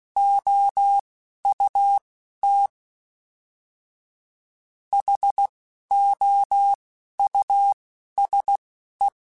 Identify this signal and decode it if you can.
Morse code, heard as a keyed tone, about 16 wpm, spaced as for 11 wpm, 800 Hz. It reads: OUT HOUSE
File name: General / General-mp3s / Houses @10